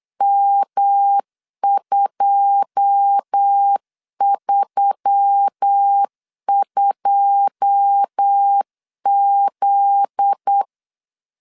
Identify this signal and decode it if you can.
Morse code, heard as a keyed tone, about 8 wpm, 800 Hz. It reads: M232Z
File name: LEZIONI / 9 2 8wpm 3437